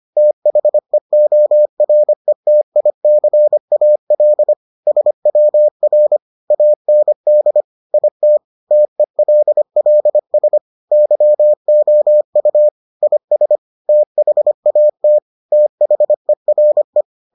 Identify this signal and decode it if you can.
Morse code, heard as a keyed tone, about 25 wpm, 600 Hz. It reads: THEORETICAL SWR AND IT TELLS YOU IS THAT THERE